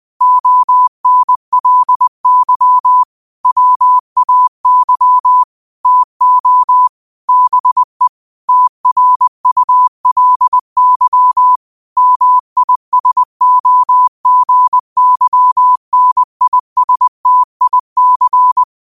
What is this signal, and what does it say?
Morse code, heard as a keyed tone, about 20 wpm, 1000 Hz. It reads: ONLY WAY TO BE TRULY MISOGYNISTIC
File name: qotdfiles / 20wpm